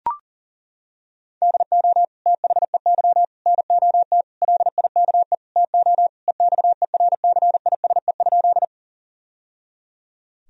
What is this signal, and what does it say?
Morse code, heard as a keyed tone, about 37 wpm, 700 Hz. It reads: DO THEY NOT LIKE TO EXERCISE?
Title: Sentences from Top 1000 Words - Review Part 4 - 40wpm